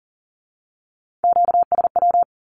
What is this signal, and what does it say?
Morse code, heard as a keyed tone, about 42 wpm, 700 Hz. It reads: QSW
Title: Q-Codes with Spoken Meaning - 40wpm